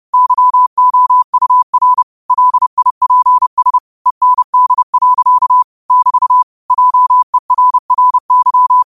Morse code, heard as a keyed tone, about 30 wpm, 1000 Hz. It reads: YOUR LIPS END. <BT> JERRY
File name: qotdfiles / 30wpm